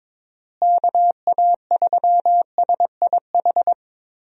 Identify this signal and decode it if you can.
Morse code, heard as a keyed tone, about 22 wpm, 700 Hz. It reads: KA3SIH
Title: USA Call Signs - 6 Characters - Repeated 3x - 22wpm